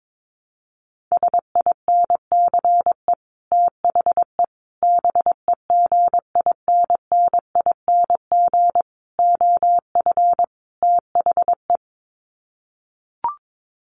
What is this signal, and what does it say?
Morse code, heard as a keyed tone, about 22 wpm, 700 Hz. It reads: SINCE THE BEGINNING OF THE
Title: Sets of 5 Words Encore2 22wpm